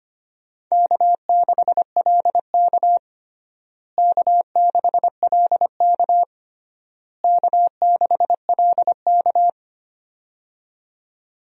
Morse code, heard as a keyed tone, about 25 wpm, 700 Hz. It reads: K6LK K6LK K6LK
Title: USA Call Signs - 4 Characters - Repeated 3x - 25wpm